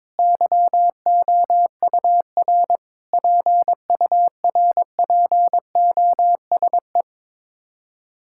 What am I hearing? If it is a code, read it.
Morse code, heard as a keyed tone, about 22 wpm, 700 Hz. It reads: YOUR PURPOSE
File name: Sets of 9 Words - 22wpm